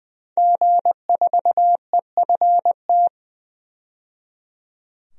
Morse code, heard as a keyed tone, about 21 wpm, 700 Hz. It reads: G4EFT
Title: DXCC-100 Call Signs - 5 Characters - 20wpm